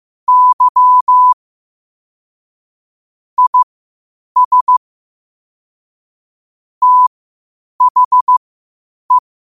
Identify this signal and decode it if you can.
Morse code, heard as a keyed tone, about 15 wpm, spaced as for 8 wpm, 1000 Hz. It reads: Y IS THE
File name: qotdfiles / 8wpm